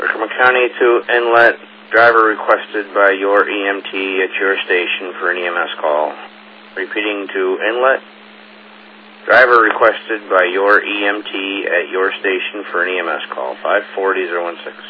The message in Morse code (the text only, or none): none